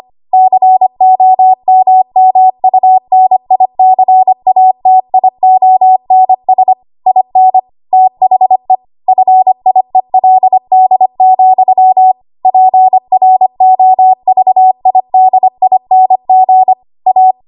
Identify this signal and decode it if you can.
Morse code, heard as a keyed tone, about 25 wpm, 750 Hz. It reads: COMMUNICATIONS IN THE FIELD, PROVIDING A
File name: Morse / 25WPM